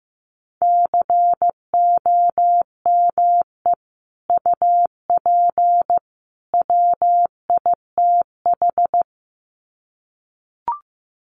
Morse code, heard as a keyed tone, about 15 wpm, 700 Hz. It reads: COME UP WITH
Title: Sets of 3 Words 15wpm